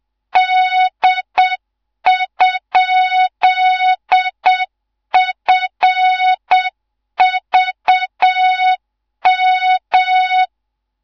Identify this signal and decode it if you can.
Morse code, heard as a keyed tone, about 7 wpm, 3700 Hz. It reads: D?FVM